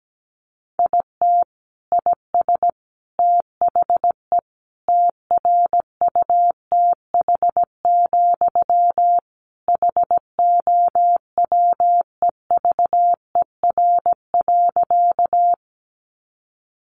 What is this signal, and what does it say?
Morse code, heard as a keyed tone, about 17 wpm, 700 Hz. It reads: IT IS THE TRUTH, HOWEVER.
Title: Sentences from Top 800 Words 17wpm